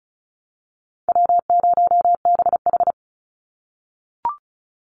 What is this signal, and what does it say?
Morse code, heard as a keyed tone, about 36 wpm, 700 Hz. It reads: W0BH
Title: Call Signs - 35wpm